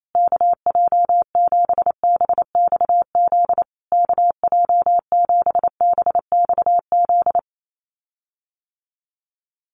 Morse code, heard as a keyed tone, about 28 wpm, 700 Hz. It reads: KJ7BXZ KJ7BXZ
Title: USA Call Signs Repeated 3x 28wpm